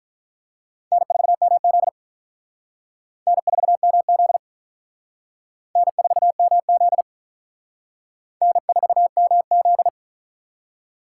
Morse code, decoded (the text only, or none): N4MZ N4MZ N4MZ N4MZ